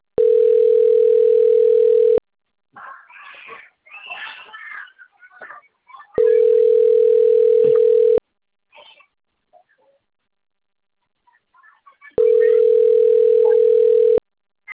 Ringback tone (North America), 440 + 480 Hz.
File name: rg-1600-unknown- - - .83892